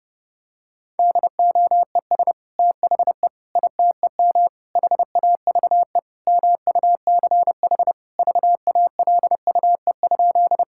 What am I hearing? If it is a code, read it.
Morse code, heard as a keyed tone, about 30 wpm, 700 Hz. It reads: DOES THE ITEM HAVE MUCH VALUE?
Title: Sentences from Top 1000 Words - Review Part 6 - 30wpm